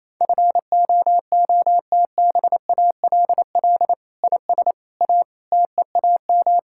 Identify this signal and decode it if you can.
Morse code, heard as a keyed tone, about 28 wpm, 700 Hz. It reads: FOOTBALL IS A TEAM
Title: Sentences from Top 1600 Words 28wpm